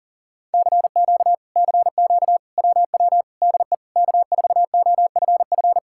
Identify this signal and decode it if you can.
Morse code, heard as a keyed tone, about 53 wpm, 700 Hz. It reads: CQ CQ WW DE K4OFF